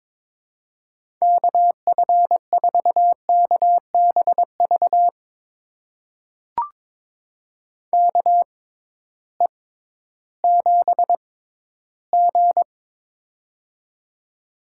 Morse code, heard as a keyed tone, about 22 wpm, spaced as for 13 wpm, 700 Hz. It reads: KF4KBV KE7G
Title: USA Call Signs - 6x Character Spacing - 22wpm